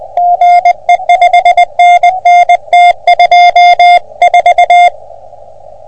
Morse code, heard as a keyed tone, about 21 wpm, 720 Hz. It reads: GE 5NN T24
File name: OM3PA PILUP CUC 388